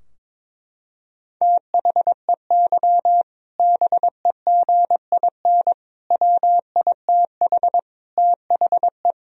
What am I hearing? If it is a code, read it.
Morse code, heard as a keyed tone, about 22 wpm, 700 Hz. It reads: THEY BEGIN WITH THE
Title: Sentences from Top 1000 Words - Review Part 3 - 22wpm